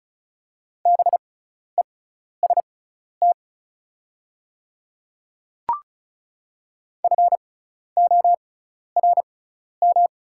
Morse code, heard as a keyed tone, about 36 wpm, spaced as for 12 wpm, 690 Hz. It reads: BEST FORM
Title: 6x Character Spacing - Top 500 Words - 35wpm